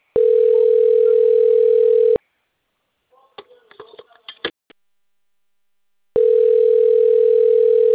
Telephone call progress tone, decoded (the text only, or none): ringback tone